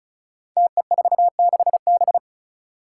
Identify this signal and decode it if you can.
Morse code, heard as a keyed tone, about 36 wpm, 700 Hz. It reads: TE46B
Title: Five Challenging Letter-Number Combinations 35wpm